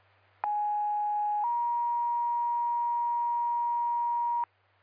Two-tone page, heard as a one-second tone, then a three-second tone, about 830 then 950 Hz. It is heard from 0.4 to 4.4 s.